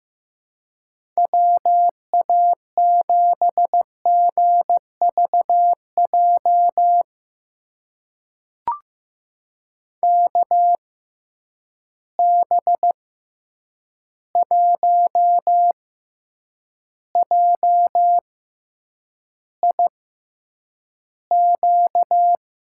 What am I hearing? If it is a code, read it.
Morse code, heard as a keyed tone, about 15 wpm, spaced as for 8 wpm, 700 Hz. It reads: WA7GVJ KB1JIQ
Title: USA Call Signs - 6 Characters - 6x Character Spacing - 15wpm